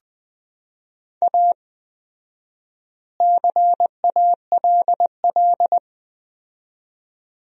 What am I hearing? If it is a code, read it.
Morse code, heard as a keyed tone, about 20 wpm, 700 Hz. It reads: A CALL
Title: 4x Word-Spacing - Sentences from Top 300 Words - 20wpm